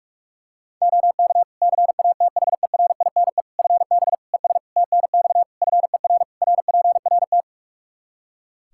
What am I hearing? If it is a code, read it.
Morse code, heard as a keyed tone, about 41 wpm, 700 Hz. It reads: OK CATHERINE FB ES TNX FER RPRT
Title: Practice QSO - Exchange 2 - Info - 45wpm